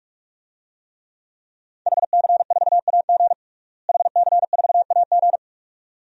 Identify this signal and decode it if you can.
Morse code, heard as a keyed tone, about 42 wpm, 700 Hz. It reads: HC4AG HC4AG
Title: DXCC-100 Call Signs - 5 Characters - Repeated 2x - 45wpm